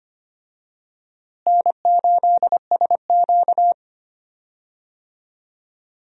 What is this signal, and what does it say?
Morse code, heard as a keyed tone, about 25 wpm, 700 Hz. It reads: N8SQ
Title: USA Call Signs - 4 Characters - Repeated 2x - 25wpm